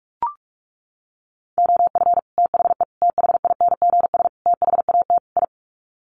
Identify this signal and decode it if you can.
Morse code, heard as a keyed tone, about 44 wpm, 700 Hz. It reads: OF THE THINGS THAT I
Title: Sets of 5 Words Encore 45wpm